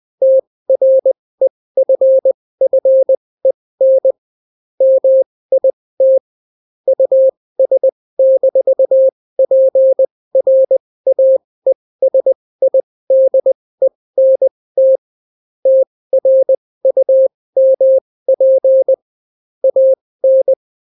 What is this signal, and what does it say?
Morse code, heard as a keyed tone, about 20 wpm, spaced as for 16 wpm, 530 Hz. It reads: TREFFEN MIT US-PRAESIDENT TRUMP AN